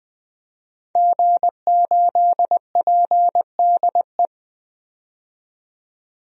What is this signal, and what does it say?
Morse code, heard as a keyed tone, about 20 wpm, 700 Hz. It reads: G8PDE
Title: DXCC-100 Call Signs - 5 Characters - 20wpm